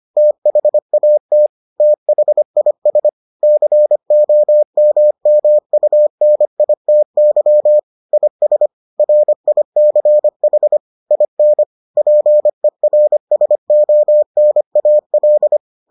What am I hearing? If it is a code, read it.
Morse code, heard as a keyed tone, about 25 wpm, 600 Hz. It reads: THAT THIS COMMUNITY IS RICH IN PERSONAL